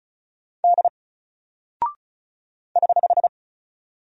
Morse code, heard as a keyed tone, about 36 wpm, 700 Hz. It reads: D <HH>
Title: Single Character (Letter:Number:Punctuation:Prosign) 35wpm